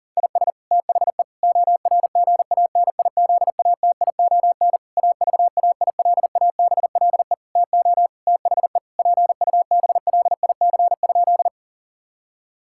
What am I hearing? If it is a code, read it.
Morse code, heard as a keyed tone, about 40 wpm, 700 Hz. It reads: IS THE ORGANIZATION AVAILABLE TO THE PUBLIC?
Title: Sentences from Top 1000 Words - Review Part 5 - 40wpm